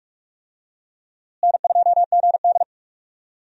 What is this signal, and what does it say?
Morse code, heard as a keyed tone, about 46 wpm, 700 Hz. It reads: N2GD
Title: USA Call Signs - 4 Characters - Repeated 2x - 45wpm